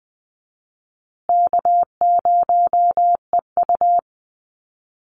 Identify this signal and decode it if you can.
Morse code, heard as a keyed tone, about 20 wpm, 700 Hz. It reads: K0EU
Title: Call Signs Repeated 3x - 20wpm